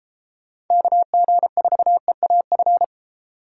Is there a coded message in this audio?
Yes: Morse code from 0.7 to 2.8 s, heard as a keyed tone, about 36 wpm, 700 Hz. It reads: KG4EAF